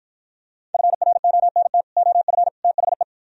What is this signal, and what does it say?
Morse code, heard as a keyed tone, about 52 wpm, 700 Hz. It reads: FRONT OF THE